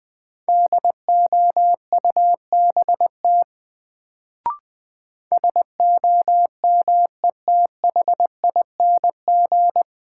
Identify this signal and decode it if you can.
Morse code, heard as a keyed tone, about 21 wpm, 700 Hz. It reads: DOUBT SOMETHING
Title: Sets of 5 Words - New Word Drill - 20wpm